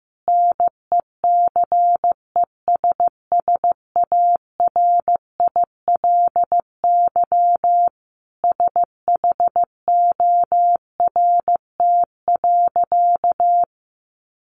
Morse code, heard as a keyed tone, about 15 wpm, 700 Hz. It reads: NECESSARILY SHORT.
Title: Sentences from Top 1400 Words 15wpm